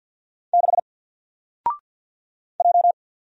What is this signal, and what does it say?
Morse code, heard as a keyed tone, about 50 wpm, 700 Hz. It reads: 6 J
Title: Single Letter-Number 50wpm